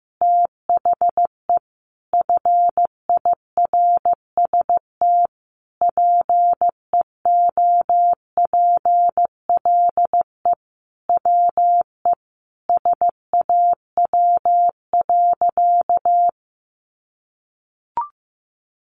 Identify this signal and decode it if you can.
Morse code, heard as a keyed tone, about 14 wpm, 700 Hz. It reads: THE FIRST PEOPLE WE SAW.
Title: Sentences from Top 300 Words 15wpm